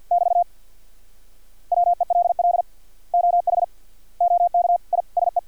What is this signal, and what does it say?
Morse code, heard as a keyed tone, about 50 wpm, 700 Hz. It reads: <BT> WELL OH OK I SE
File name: archived / CRSnet